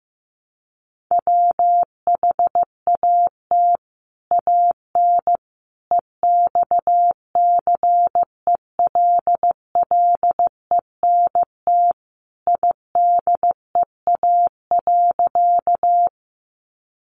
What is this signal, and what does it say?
Morse code, heard as a keyed tone, about 15 wpm, 700 Hz. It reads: WHAT AN EXCELLENT IDEA.
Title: Sentences from Top 1000 Words 15wpm